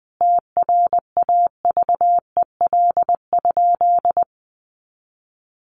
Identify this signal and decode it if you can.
Morse code, heard as a keyed tone, about 20 wpm, 700 Hz. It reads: TRAVEL?
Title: Sentences from Top 1400 Words 20wpm